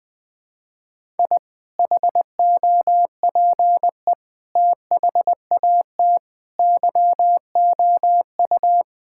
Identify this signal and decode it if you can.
Morse code, heard as a keyed tone, about 20 wpm, 700 Hz. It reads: I HOPE THAT YOU
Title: Sentences with 4-Word Phrases - 20wpm